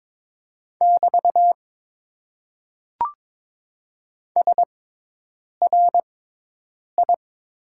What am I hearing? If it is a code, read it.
Morse code, heard as a keyed tone, about 22 wpm, spaced as for 8 wpm, 700 Hz. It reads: <BT> SRI
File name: Top 90 QSO Elements - 6x Character Spacing - 22wpm